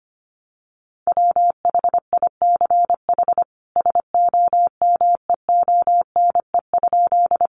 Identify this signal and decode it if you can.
Morse code, heard as a keyed tone, about 25 wpm, 700 Hz. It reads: WHICH SOMEONE?
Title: Sentences from Top 600 Words 25wpm